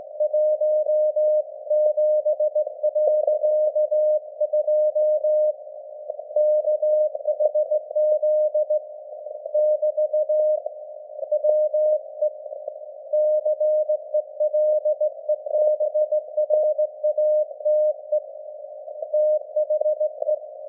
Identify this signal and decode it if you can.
Morse code, heard as a keyed tone, about 18 wpm, 610 Hz. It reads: 17.2 KHZ <BT> WE CELEBRATE THE
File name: laengstwellenempfang / SDRSharp 17kHz AF